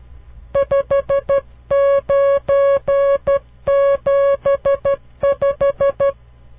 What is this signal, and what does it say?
Morse code, heard as a keyed tone, about 12 wpm, 560 Hz. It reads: 5975
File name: CW Learning Programs / farnsworth / unit 5